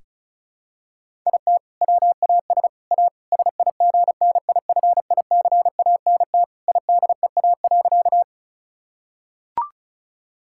Morse code, heard as a keyed tone, about 36 wpm, 700 Hz. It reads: IT WAS A SIGNIFICANT IDEA.
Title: Sentences from Top 1000 Words - Review Part 6 - 35wpm